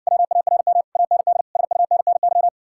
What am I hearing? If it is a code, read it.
Morse code, heard as a keyed tone, about 60 wpm, 700 Hz. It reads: WARM AND SUNNY